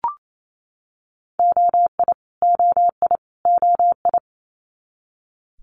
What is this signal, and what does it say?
Morse code, heard as a keyed tone, about 28 wpm, 700 Hz. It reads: OI OI OI
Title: Multi-letter Phonemes Repeated 3x 28wpm